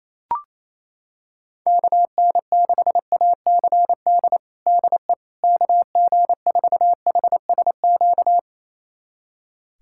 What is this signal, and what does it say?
Morse code, heard as a keyed tone, about 28 wpm, 700 Hz. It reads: KN6ACD DE KG4HSQ
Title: USA Callsign DE Callsign 28wpm